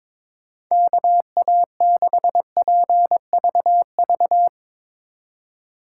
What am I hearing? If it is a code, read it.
Morse code, heard as a keyed tone, about 22 wpm, 700 Hz. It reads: KA6PVV
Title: USA Call Signs - 6 Characters - 4x Character Spacing - 22wpm